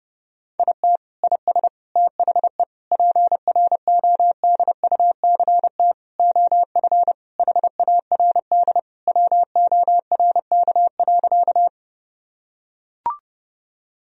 Morse code, heard as a keyed tone, about 30 wpm, 700 Hz. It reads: IT IS THE PRODUCT OF HARD WORK.
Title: Sentences from Top 1000 Words - Review Part 2 - 30wpm